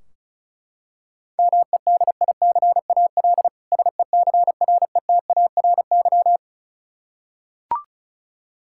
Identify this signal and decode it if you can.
Morse code, heard as a keyed tone, about 36 wpm, 700 Hz. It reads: MEDICAL SECRETARY
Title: K1USN SST QSO Practice 35wpm